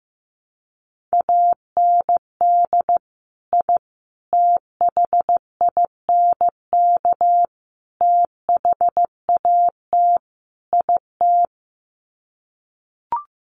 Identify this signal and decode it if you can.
Morse code, heard as a keyed tone, about 15 wpm, 700 Hz. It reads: AND I THINK THAT IT
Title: Sets of 5 Words Encore2 15wpm